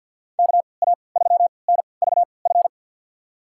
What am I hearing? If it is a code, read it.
Morse code, heard as a keyed tone, about 52 wpm, spaced as for 29 wpm, 690 Hz. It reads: KA3NVF
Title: USA Call Signs - 6 Characters - 3x Character Spacing - 50wpm